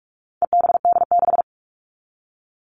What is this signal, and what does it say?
Morse code, heard as a keyed tone, about 44 wpm, 700 Hz. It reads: EBD6
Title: All Cut Numbers - 4-Digit 45wpm